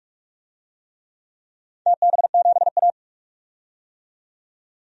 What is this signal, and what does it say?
Morse code, heard as a keyed tone, about 44 wpm, 700 Hz. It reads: TB7A
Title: DXCC-100 Call Signs - 4 Characters - 45wpm